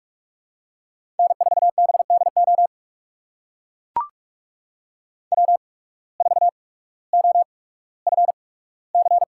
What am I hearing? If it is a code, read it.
Morse code, heard as a keyed tone, about 44 wpm, spaced as for 19 wpm, 700 Hz. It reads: N4BDO W4OFC
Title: USA Call Signs - 5 Characters - 8x Character Spacing - 45wpm